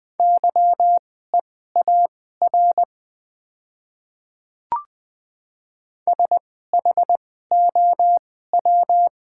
Morse code, heard as a keyed tone, about 20 wpm, spaced as for 14 wpm, 700 Hz. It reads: YEAR SHOW
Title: 2x Character Spacing - Top 200 Words - 20wpm